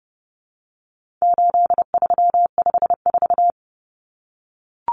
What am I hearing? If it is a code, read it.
Morse code, heard as a keyed tone, about 30 wpm, 700 Hz. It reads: 8354
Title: Numbers - 4-Digit 30wpm